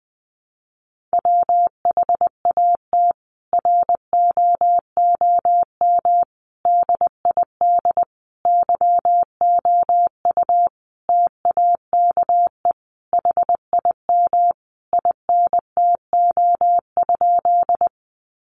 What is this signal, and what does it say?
Morse code, heard as a keyed tone, about 20 wpm, 700 Hz. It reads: WHAT ROOM DID YOU TAKE HIM INTO?
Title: Sentences from Top 600 Words 20wpm